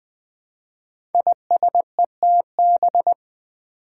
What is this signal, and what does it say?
Morse code, heard as a keyed tone, about 20 wpm, 700 Hz. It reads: ISETB
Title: Five Challenging Letter-Number Combinations 20wpm